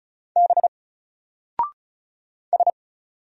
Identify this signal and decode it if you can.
Morse code, heard as a keyed tone, about 35 wpm, 700 Hz. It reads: B S
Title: Single Letter-Number 35wpm